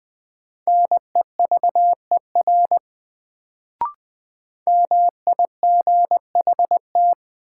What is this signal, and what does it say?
Morse code, heard as a keyed tone, about 20 wpm, 700 Hz. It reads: NEVER MIGHT
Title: Sentences from Top 300 Words - New Word Drill - 20wpm